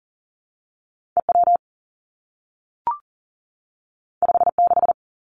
Morse code, heard as a keyed tone, about 38 wpm, 700 Hz. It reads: E W 5 6
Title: Two Challenging Letter-Number Combinations 40wpm